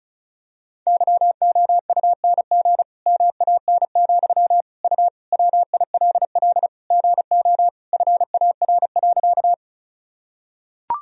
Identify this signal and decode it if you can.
Morse code, heard as a keyed tone, about 36 wpm, 700 Hz. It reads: YOUNG MAN, U WILL GO FAR.
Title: Sentences from Top 400 Words and Abbreviations 35wpm